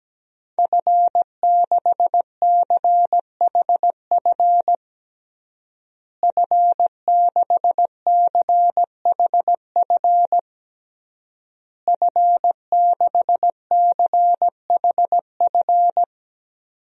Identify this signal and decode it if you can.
Morse code, heard as a keyed tone, about 17 wpm, 700 Hz. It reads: F6CHF F6CHF F6CHF
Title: DXCC-100 Call Signs - 5 Characters - Repeated 3x - 17wpm